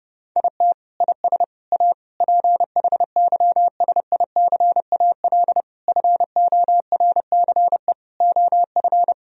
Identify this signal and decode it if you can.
Morse code, heard as a keyed tone, about 30 wpm, 700 Hz. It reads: IT IS A PHYSICAL FORCE OF
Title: Sentences from Top 1000 Words - Review Part 4 - 30wpm